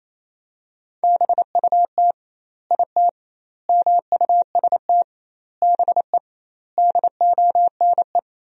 Morse code, heard as a keyed tone, about 28 wpm, 700 Hz. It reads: BUT IT MUST BE DONE
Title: 2x Word-Spacing - Sentences from Top 500 Words - 28wpm